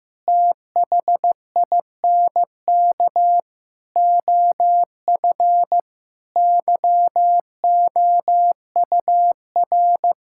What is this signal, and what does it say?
Morse code, heard as a keyed tone, about 14 wpm, 700 Hz. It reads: THINK OF YOUR
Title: Sentences from Top 1000 Words - Review Part 5 - 15wpm